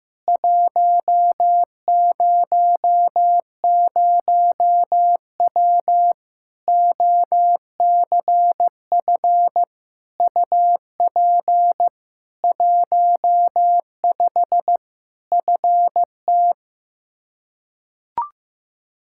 Morse code, heard as a keyed tone, about 15 wpm, 700 Hz. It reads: 100W OCF UP 15 FT
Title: Practice QSO - Exchange 2 - Info - 15wpm